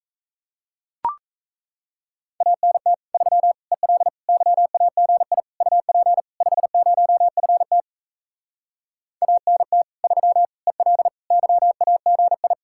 Morse code, heard as a keyed tone, about 38 wpm, 700 Hz. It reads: ANT 3 EL YAGI UP 50FT ANT 3 EL YAGI